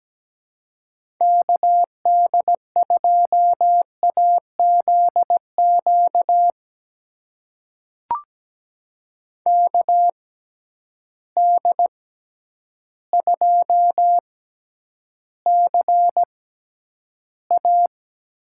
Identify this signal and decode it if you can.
Morse code, heard as a keyed tone, about 17 wpm, spaced as for 9 wpm, 700 Hz. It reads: KD2AZQ KD2CA
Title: USA Call Signs - 6 Characters - 6x Character Spacing - 17wpm